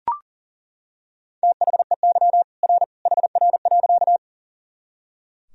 Morse code, heard as a keyed tone, about 37 wpm, 700 Hz. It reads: THEY R HR.
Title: Sentences from Top 200 Words and Abbreviations 40wpm